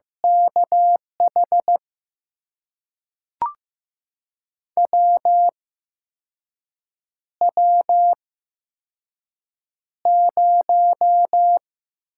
Morse code, heard as a keyed tone, about 15 wpm, spaced as for 5 wpm, 700 Hz. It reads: KH WW0
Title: USA Call Signs - 8x Character Spacing - 15wpm